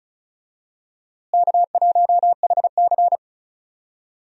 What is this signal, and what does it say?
Morse code, heard as a keyed tone, about 36 wpm, 700 Hz. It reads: K1HC
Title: USA Call Signs - 4 Characters - Repeated 2x - 35wpm